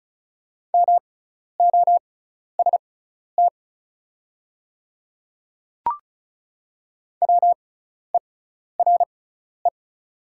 Morse code, heard as a keyed tone, about 34 wpm, spaced as for 12 wpm, 690 Hz. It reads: MOST WERE
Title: 6x Character Spacing - Top 100 Words - 35wpm